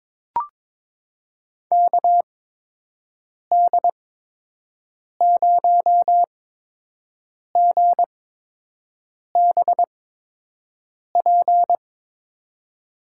Morse code, heard as a keyed tone, about 22 wpm, spaced as for 6 wpm, 700 Hz. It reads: KD0GBP
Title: USA Call Signs - 8x Character Spacing - 22wpm